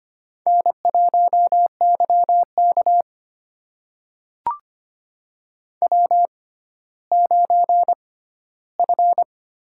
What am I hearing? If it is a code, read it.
Morse code, heard as a keyed tone, about 25 wpm, spaced as for 14 wpm, 700 Hz. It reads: N1YK W9F
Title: USA Call Signs - 4 Characters - 6x Character Spacing - 25wpm